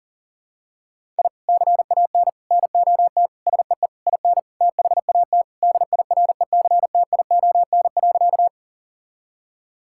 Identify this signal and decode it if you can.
Morse code, heard as a keyed tone, about 40 wpm, 700 Hz. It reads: I CAN NOT SEE IN THAT DIRECTION.
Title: Sentences from Top 1000 Words - Review Part 5 - 40wpm